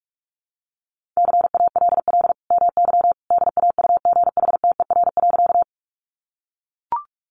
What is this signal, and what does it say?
Morse code, heard as a keyed tone, about 44 wpm, 700 Hz. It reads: CALL MY DAUGHTER.
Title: Sentences from Top 1500 Words 45wpm